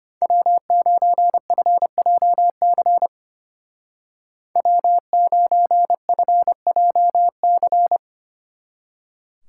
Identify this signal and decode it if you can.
Morse code, heard as a keyed tone, about 33 wpm, 700 Hz. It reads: W9FJC W9FJC